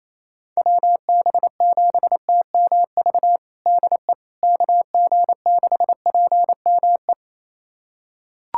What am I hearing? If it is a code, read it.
Morse code, heard as a keyed tone, about 28 wpm, 700 Hz. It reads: WB7TMV DE KG6PME
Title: USA Callsign DE Callsign 28wpm